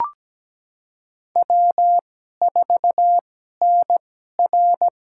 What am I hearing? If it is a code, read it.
Morse code, heard as a keyed tone, about 17 wpm, spaced as for 12 wpm, 700 Hz. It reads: W4NR
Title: USA Call Signs - 4 Characters - 2x Character Spacing - 17wpm